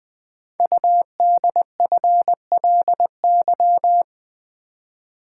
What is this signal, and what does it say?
Morse code, heard as a keyed tone, about 20 wpm, 700 Hz. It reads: UDFLY
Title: Code Groups - 5 Letters - 20wpm